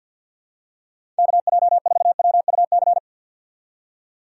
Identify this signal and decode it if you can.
Morse code, heard as a keyed tone, about 52 wpm, 700 Hz. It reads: KJ4WUC
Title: USA Call Signs - 6 Characters - 3x Character Spacing - 50wpm